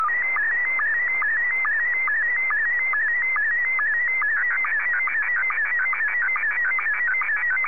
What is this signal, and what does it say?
SSTV (slow-scan television), Scottie 1.